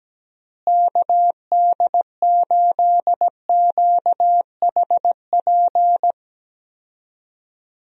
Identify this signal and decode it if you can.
Morse code, heard as a keyed tone, about 17 wpm, 700 Hz. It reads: KD8QHP
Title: USA Call Signs - 6 Characters - Repeated 2x - 17wpm